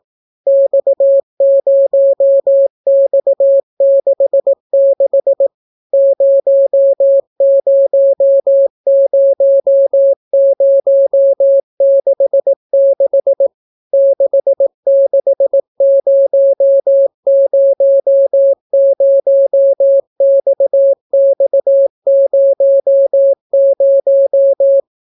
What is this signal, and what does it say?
Morse code, heard as a keyed tone, about 18 wpm, 550 Hz. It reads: X0X66 000066 66000XX00